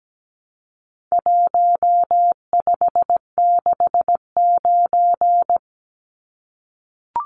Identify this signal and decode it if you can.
Morse code, heard as a keyed tone, about 17 wpm, 700 Hz. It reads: 1569
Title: Numbers - Contest Style - 4-Digit 17wpm